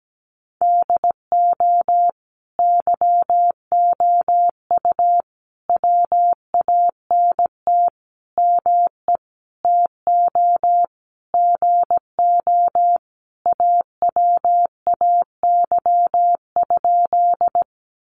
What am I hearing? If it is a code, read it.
Morse code, heard as a keyed tone, about 17 wpm, 700 Hz. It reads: DO YOU WANT ME TO GO AWAY?
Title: Sentences from Top 600 Words 17wpm